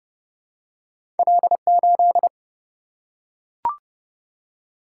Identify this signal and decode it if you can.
Morse code, heard as a keyed tone, about 30 wpm, 700 Hz. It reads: L 8
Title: Single-Letter Single-Number 30wpm